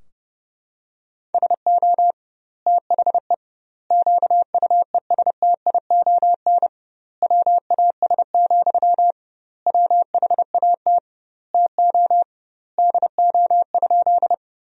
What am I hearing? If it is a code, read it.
Morse code, heard as a keyed tone, about 30 wpm, 700 Hz. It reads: SO THE QUESTION WAS, WHAT TO DO?
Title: 2x Word-Spacing - Sentences from Top 400 Words - 30wpm